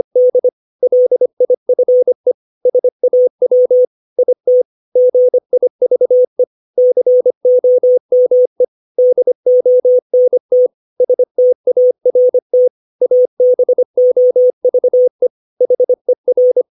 Morse code, heard as a keyed tone, about 25 wpm, 500 Hz. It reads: D LIFE SAW IT GIVE COME DONT START ABOVE HER
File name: 632 words / 632 Words 25WPM